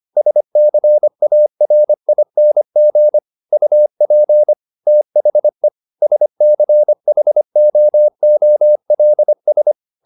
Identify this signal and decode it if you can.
Morse code, heard as a keyed tone, about 25 wpm, 600 Hz. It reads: SCARING UP THE SCHOOLS